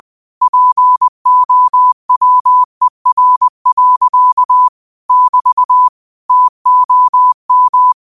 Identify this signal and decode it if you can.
Morse code, heard as a keyed tone, about 20 wpm, 1000 Hz. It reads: POWER. <BT> TOM
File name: qotdfiles / 20wpm